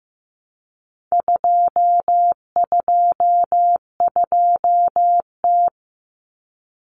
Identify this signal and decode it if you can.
Morse code, heard as a keyed tone, about 15 wpm, 700 Hz. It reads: 222T
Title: Common Cut Numbers - 4-Digit 15wpm